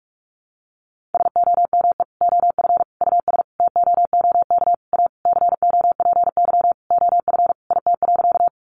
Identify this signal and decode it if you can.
Morse code, heard as a keyed tone, about 44 wpm, 700 Hz. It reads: SOME OF US TOOK A COPY OF IT.
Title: Sentences from Top 1400 Words 45wpm